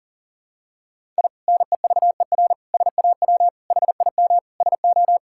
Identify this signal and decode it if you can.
Morse code, heard as a keyed tone, about 40 wpm, 700 Hz. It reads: I NEVER SAW HIM SO
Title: Sentences from Top 1000 Words - Review Part 6 - 40wpm